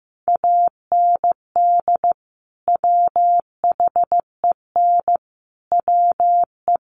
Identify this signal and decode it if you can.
Morse code, heard as a keyed tone, about 15 wpm, 700 Hz. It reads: AND WHEN WE
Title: Sets of 5 Words 15wpm